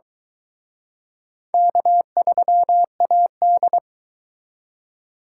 Morse code, heard as a keyed tone, about 23 wpm, 700 Hz. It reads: K3AD